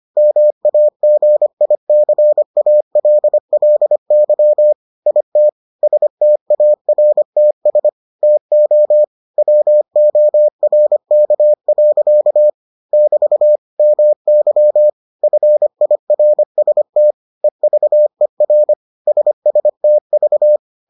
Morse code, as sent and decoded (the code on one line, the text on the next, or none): -- .- --. .. -.-. .- .-.. .-.. -.-- / .. - / ... - .- .-. - ... / - --- / .-- --- .-. -.- .-.-.- / -...- / -- -.-- / ..-. .. .-. ... - / . ...- . .-. / ... ... - ...-
MAGICALLY IT STARTS TO WORK. <BT> MY FIRST EVER SSTV